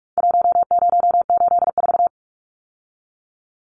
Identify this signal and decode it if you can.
Morse code, heard as a keyed tone, about 44 wpm, 700 Hz. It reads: 1084
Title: Numbers - Contest Style - 4-Digit 45wpm